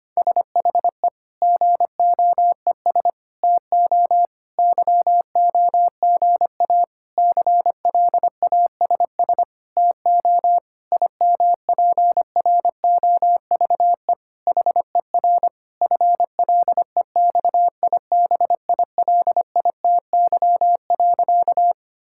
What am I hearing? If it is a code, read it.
Morse code, heard as a keyed tone, about 25 wpm, 700 Hz. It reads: SHE GOES TO YOGA CLASS TO IMPROVE HER FLEXIBILITY.
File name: practice / 25wpm / Difficult Sentences with Hints - 25wpm